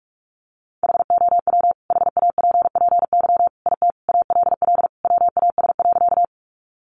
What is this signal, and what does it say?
Morse code, heard as a keyed tone, about 44 wpm, 700 Hz. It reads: HOW HAPPY IT ALL WAS.
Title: Sentences from Top 1000 Words 45wpm